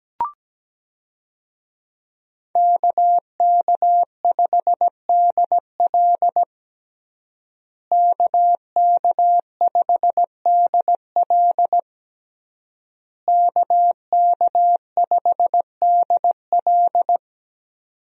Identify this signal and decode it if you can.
Morse code, heard as a keyed tone, about 17 wpm, 700 Hz. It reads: KK5DL KK5DL KK5DL
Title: USA Call Signs - 5 Characters - Repeated 3x - 17wpm